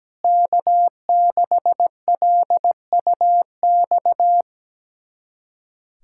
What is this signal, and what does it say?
Morse code, heard as a keyed tone, about 17 wpm, 700 Hz. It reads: K6LUX
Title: USA Callsign DE Callsign 17wpm